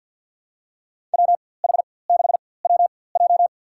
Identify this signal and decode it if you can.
Morse code, heard as a keyed tone, about 53 wpm, spaced as for 24 wpm, 700 Hz. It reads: WH6WJ
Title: USA Call Signs - 5 Characters - 4x Character Spacing - 50wpm